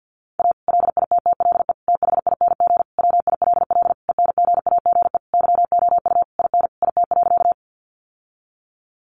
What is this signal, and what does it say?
Morse code, heard as a keyed tone, about 50 wpm, 700 Hz. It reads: A LITTLE THING WILL ENGAGE YOU IN IT.
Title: Sentences from Top 1500 Words 50wpm